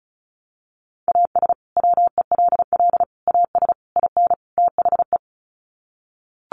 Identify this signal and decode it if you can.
Morse code, heard as a keyed tone, about 34 wpm, 700 Hz. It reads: AS WELL AS IN THE
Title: Sets of 5 Words Encore 35wpm